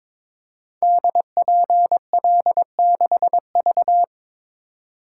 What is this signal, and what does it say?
Morse code, heard as a keyed tone, about 22 wpm, 690 Hz. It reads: DPL6V
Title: Code Groups - 5 Letter:Numbers - 22wpm